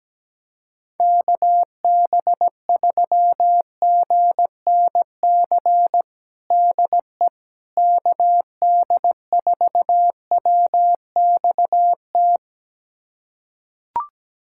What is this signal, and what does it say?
Morse code, heard as a keyed tone, about 17 wpm, 700 Hz. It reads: KB3GNC DE KD4WXT
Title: USA Callsign DE Callsign - 6 Characters - 17wpm